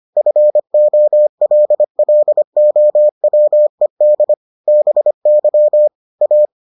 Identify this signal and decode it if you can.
Morse code, heard as a keyed tone, about 25 wpm, 600 Hz. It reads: FOLLOWED BY A